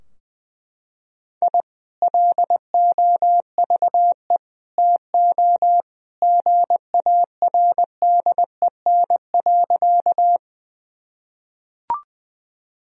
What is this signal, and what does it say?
Morse code, heard as a keyed tone, about 20 wpm, 700 Hz. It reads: I LOVE TO GARDEN.
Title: Sentences from Top 1000 Words - Review Part 4 - 20wpm